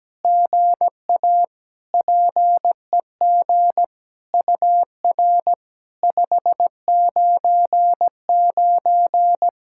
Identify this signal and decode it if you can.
Morse code, heard as a keyed tone, about 17 wpm, 700 Hz. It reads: GA PEG UR 599
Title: SOTA QSO Practice 17wpm